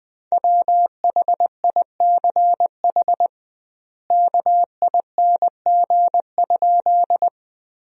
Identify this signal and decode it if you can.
Morse code, heard as a keyed tone, about 20 wpm, 700 Hz. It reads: WHICH KING?
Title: 2x Word-Spacing - Sentences from Top 400 Words - 20wpm